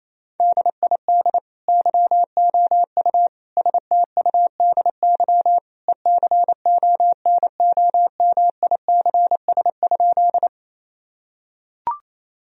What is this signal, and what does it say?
Morse code, heard as a keyed tone, about 28 wpm, 700 Hz. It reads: DID YOU STUDY ECONOMICS?
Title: Sentences from Top 1000 Words - Review Part 3 - 28wpm